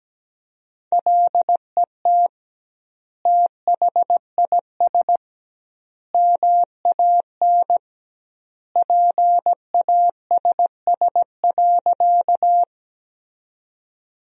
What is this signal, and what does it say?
Morse code, heard as a keyed tone, about 17 wpm, 700 Hz. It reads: LET THIS MAN PASS.
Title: 2x Word-Spacing - Sentences from Top 400 Words - 17wpm